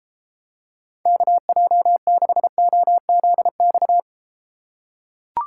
Morse code, heard as a keyed tone, about 33 wpm, 700 Hz. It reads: KJ6OZX